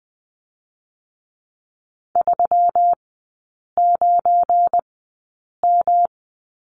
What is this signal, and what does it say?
Morse code, heard as a keyed tone, about 20 wpm, spaced as for 8 wpm, 700 Hz. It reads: 39M
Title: 172 Lesson 28e - Number 9 - Three Characters 20wpm